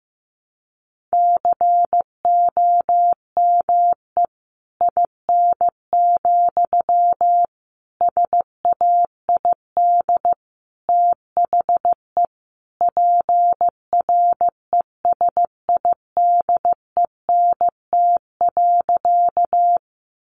Morse code, heard as a keyed tone, about 15 wpm, 700 Hz. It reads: COME IN, SAID THE PRESIDENT.
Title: Sentences from Top 1000 Words 15wpm